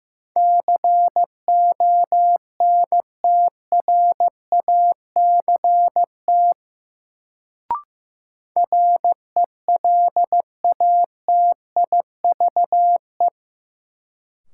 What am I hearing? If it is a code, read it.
Morse code, heard as a keyed tone, about 15 wpm, 700 Hz. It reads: CONTRACT RELATIVE
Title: Sentences from Top 1400 Words - New Word Drill Alt - 15wpm